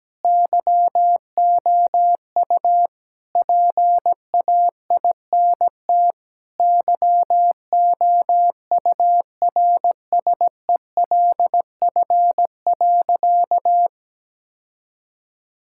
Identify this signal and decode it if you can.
Morse code, heard as a keyed tone, about 17 wpm, 700 Hz. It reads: YOU PAINT YOURSELF.
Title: Sentences from Top 1000 Words - Review Part 6 - 17wpm